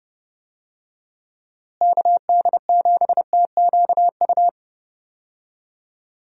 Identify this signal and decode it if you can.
Morse code, heard as a keyed tone, about 30 wpm, 700 Hz. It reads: KD7TQU
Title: USA Call Signs - 6 Characters - Repeated 3x - 30wpm